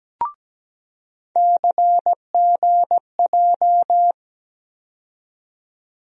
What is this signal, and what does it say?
Morse code, heard as a keyed tone, about 17 wpm, 700 Hz. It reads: CGJ
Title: USA Call Sign Postfixes - 3 Letters 17wpm